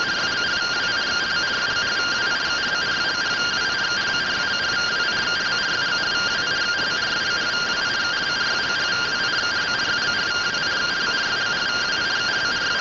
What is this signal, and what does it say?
Two-tone teleprinter signal, about 170 Hz shift.